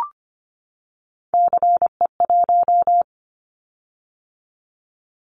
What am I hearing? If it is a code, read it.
Morse code, heard as a keyed tone, about 25 wpm, 700 Hz. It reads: CE1
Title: Callsign Prefixes for Top 100 DXCC - 25wpm